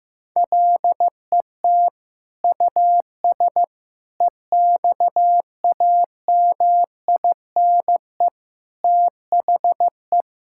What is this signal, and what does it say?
Morse code, heard as a keyed tone, about 15 wpm, 700 Hz. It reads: LET US EXAMINE THE
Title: Sentences from Top 1600 Words 15wpm